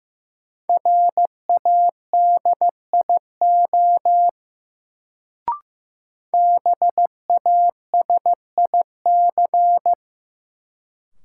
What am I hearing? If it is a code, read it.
Morse code, heard as a keyed tone, about 15 wpm, 700 Hz. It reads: RADIO BASIC
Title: Sentences from Top 1000 Words - Review Part 3 - New Word Drill - 15wpm